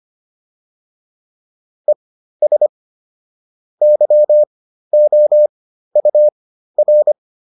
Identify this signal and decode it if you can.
Morse code, heard as a keyed tone, about 25 wpm, spaced as for 13 wpm, 600 Hz. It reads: ES YOUR
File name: QSO307 13